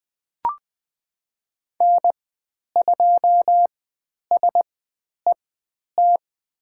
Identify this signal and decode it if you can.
Morse code, heard as a keyed tone, about 20 wpm, spaced as for 10 wpm, 700 Hz. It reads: N2SET
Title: 102 Lesson 17g - Letter u - Call Signs 20wpm